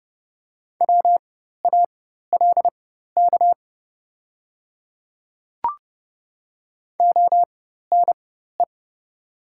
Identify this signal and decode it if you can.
Morse code, heard as a keyed tone, about 30 wpm, spaced as for 14 wpm, 700 Hz. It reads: WALK ONE
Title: 4x Character Spacing - Top 300 Words - 30wpm